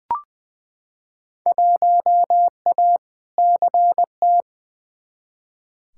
Morse code, heard as a keyed tone, about 20 wpm, 700 Hz. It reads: 1A CT
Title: Field Day Exchanges Encore - 20wpm